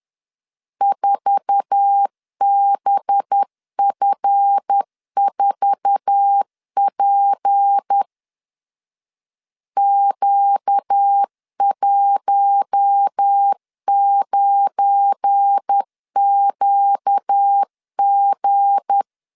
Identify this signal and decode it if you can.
Morse code, heard as a keyed tone, about 10 wpm, 800 Hz. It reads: 4BF4P Q19QG
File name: LEZIONI / 8 YJ45 / 8 YJ4 10wpm 2633